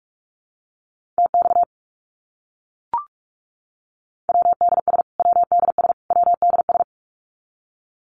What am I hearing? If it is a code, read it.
Morse code, heard as a keyed tone, about 44 wpm, 700 Hz. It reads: TX WDS WDS WDS
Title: Abbreviations Repeated 3x - 45wpm